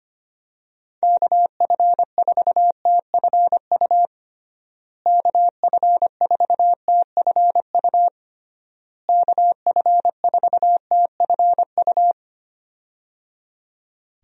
Morse code, heard as a keyed tone, about 25 wpm, 700 Hz. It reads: KF4TFU KF4TFU KF4TFU
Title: USA Call Signs - 6 Characters - Repeated 3x - 25wpm